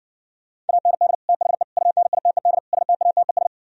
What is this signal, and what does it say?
Morse code, heard as a keyed tone, about 60 wpm, 700 Hz. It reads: AND THE UNITED STATES